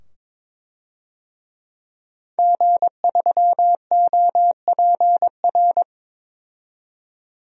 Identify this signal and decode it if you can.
Morse code, heard as a keyed tone, about 22 wpm, 700 Hz. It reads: G3OPR
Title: DXCC-100 Call Signs - 5 Characters - 22wpm